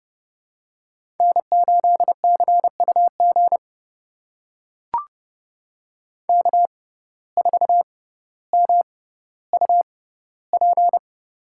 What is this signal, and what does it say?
Morse code, heard as a keyed tone, about 30 wpm, spaced as for 15 wpm, 700 Hz. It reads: N8CUG K4MUP
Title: USA Call Signs - 5 Characters - 6x Character Spacing - 30wpm